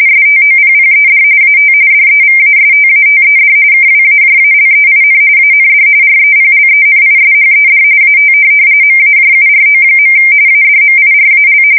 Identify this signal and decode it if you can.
RTTY, about 45.45 baud. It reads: STOPPED AT DEVASTATED KWAJALEIN AND REPLENISHED AT MAJURO. ADMIRAL
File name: fsk170 / USS-New-Jersey